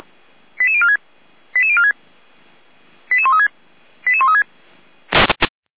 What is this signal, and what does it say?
Five-tone selective call (ZVEI), heard as quick runs of five tones, tones of about 70 ms. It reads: 88046 88046 88136 88136